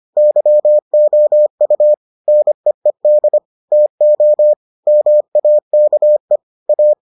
Morse code, heard as a keyed tone, about 25 wpm, 600 Hz. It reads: YOU NEED TO MAKE A